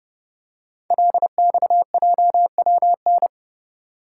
Morse code, heard as a keyed tone, about 29 wpm, 700 Hz. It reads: LXJWN
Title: Code Groups - 5 Letter:Numbers - 30wpm